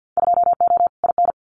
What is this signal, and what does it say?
Morse code, heard as a keyed tone, about 50 wpm, 700 Hz. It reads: 2O IN
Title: Winter Field Day Exchanges 50wpm